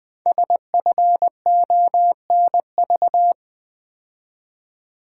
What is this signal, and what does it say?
Morse code, heard as a keyed tone, about 20 wpm, 700 Hz. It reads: SFONV
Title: Code Groups - 5 Letters - 20wpm